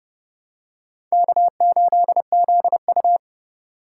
Morse code, heard as a keyed tone, about 29 wpm, 700 Hz. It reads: K8ZU